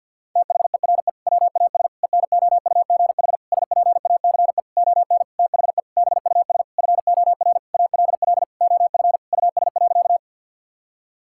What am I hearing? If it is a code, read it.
Morse code, heard as a keyed tone, about 50 wpm, 700 Hz. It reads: THERE WAS ENOUGH SPACE ON THE BUS FOR ALL OF US.
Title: Sentences from Top 1000 Words - Review Part 6 - 50wpm